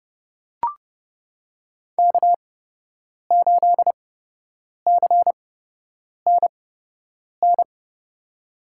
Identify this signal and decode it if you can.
Morse code, heard as a keyed tone, about 30 wpm, spaced as for 8 wpm, 700 Hz. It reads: K8CNN
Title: USA Call Signs - 8x Character Spacing - 30wpm